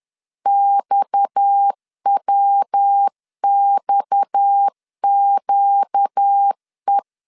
Morse code, heard as a keyed tone, about 10 wpm, 800 Hz. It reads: XWXQE
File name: LEZIONI / 6 PKX0 / 6 PKX 10wpm 2426